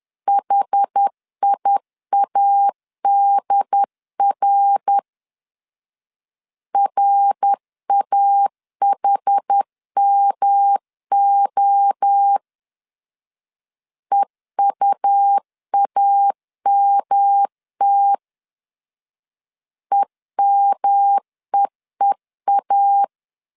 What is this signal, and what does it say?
Morse code, heard as a keyed tone, about 10 wpm, 800 Hz. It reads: HIADR RAHMO EUAMT EMEEA
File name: CW / DET / 1 3 1500C S C 10wpm 3335